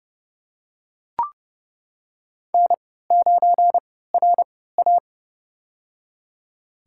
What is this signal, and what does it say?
Morse code, heard as a keyed tone, about 30 wpm, spaced as for 17 wpm, 700 Hz. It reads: N9RA
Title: USA Call Signs - 4 Characters - 3x Character Spacing - 30wpm